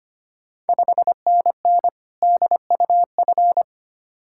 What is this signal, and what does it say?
Morse code, heard as a keyed tone, about 25 wpm, 690 Hz. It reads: 5NN DUF
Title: Ontario QSO Party Exchange 25wpm